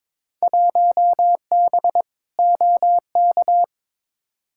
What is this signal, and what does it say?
Morse code, heard as a keyed tone, about 21 wpm, 700 Hz. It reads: 1B OK
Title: Field Day Exchanges Encore - 22wpm